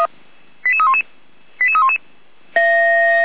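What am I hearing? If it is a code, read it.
Five-tone selective call (ZVEI), heard as quick runs of five tones, tones of about 70 ms. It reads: 88311 88311
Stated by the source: ZVEI five-tone selective call received by SDR.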